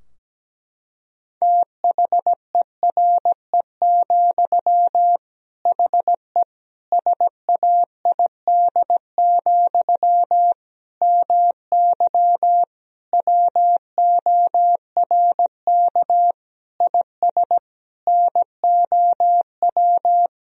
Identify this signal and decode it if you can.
Morse code, heard as a keyed tone, about 17 wpm, 700 Hz. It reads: THERE, HE SAID, MY WORK IS NOW
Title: Sentences from Top 1000 Words - Review Part 2 - 17wpm